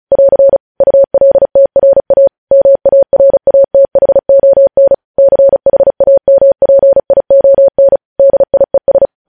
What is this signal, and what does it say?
Morse code, heard as a keyed tone, about 36 wpm, 550 Hz. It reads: <AR> ULTRA MARATHON CHAMPION DIES